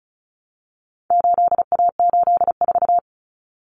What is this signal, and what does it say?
Morse code, heard as a keyed tone, about 34 wpm, 700 Hz. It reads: 8A84
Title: Common Cut Numbers - 4-Digit 35wpm